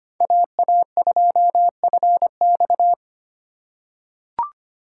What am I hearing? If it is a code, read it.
Morse code, heard as a keyed tone, about 25 wpm, 700 Hz. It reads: AA2FX
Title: USA Call Signs - 5 Characters - 4x Character Spacing - 25wpm